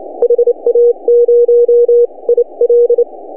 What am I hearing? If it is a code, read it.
Morse code, heard as a keyed tone, about 29 wpm, spaced as for 25 wpm, 480 Hz. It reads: HA0IL
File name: CW / HA2OS RHOB 2017 CW